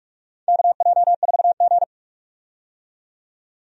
Morse code, heard as a keyed tone, about 46 wpm, 700 Hz. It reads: KJ4G
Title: USA Call Signs - 4 Characters - 45wpm